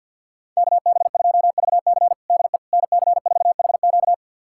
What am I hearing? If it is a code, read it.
Morse code, heard as a keyed tone, about 47 wpm, 700 Hz. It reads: KB2VC DE NC4HQ
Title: USA Callsign DE Callsign - 5 Characters - 50wpm